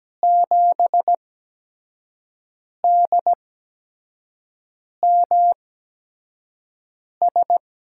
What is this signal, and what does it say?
Morse code, heard as a keyed tone, about 17 wpm, spaced as for 5 wpm, 700 Hz. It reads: 7DMS
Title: USA Call Signs - 8x Character Spacing - 17wpm